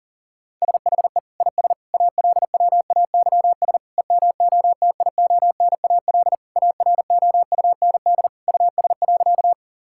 Morse code, heard as a keyed tone, about 38 wpm, 700 Hz. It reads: SHE IS ALWAYS EMOTIONAL AROUND US.
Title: Sentences from Top 1000 Words - Review Part 5 - 40wpm